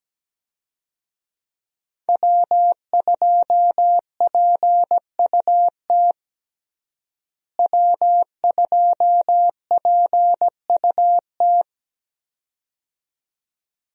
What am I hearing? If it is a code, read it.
Morse code, heard as a keyed tone, about 17 wpm, 700 Hz. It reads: W2PUT W2PUT
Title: USA Call Signs - 5 Characters - Repeated 2x - 17wpm